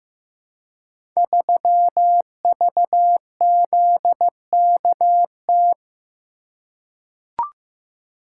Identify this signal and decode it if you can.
Morse code, heard as a keyed tone, about 15 wpm, 700 Hz. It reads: 3VZKT
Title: Code Groups - 5 Letter:Numbers - 15wpm